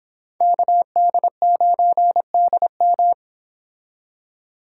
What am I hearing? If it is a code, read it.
Morse code, heard as a keyed tone, about 29 wpm, 700 Hz. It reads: KD9DM